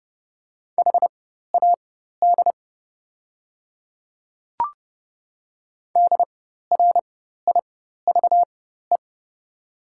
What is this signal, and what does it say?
Morse code, heard as a keyed tone, about 30 wpm, spaced as for 14 wpm, 700 Hz. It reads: HAD DRIVE
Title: 4x Character Spacing - Top 500 Words - 30wpm